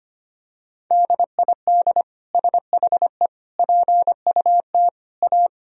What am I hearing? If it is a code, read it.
Morse code, heard as a keyed tone, about 25 wpm, 700 Hz. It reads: DID SHE PUT A
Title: Sentences from Top 1000 Words - Review Part 1 - 25wpm